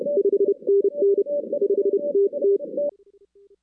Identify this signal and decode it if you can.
Morse code, heard as a keyed tone, about 32 wpm, spaced as for 27 wpm, 400 Hz. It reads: 5NN 5TT
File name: 2016 arrl dx cw / 2016-ARRL-DX-CW-K5TR- - 0-41